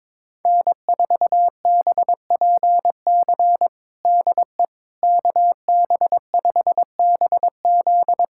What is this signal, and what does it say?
Morse code, heard as a keyed tone, about 22 wpm, 700 Hz. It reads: N4BPC DE KB5BZ
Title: USA Callsign DE Callsign - 5 Characters - 22wpm